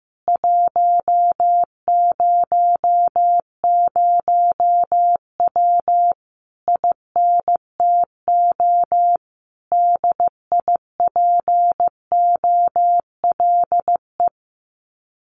Morse code, heard as a keyed tone, about 15 wpm, 700 Hz. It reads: 100W INTO DIPOLE
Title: Common QSO Phrases Encore 15wpm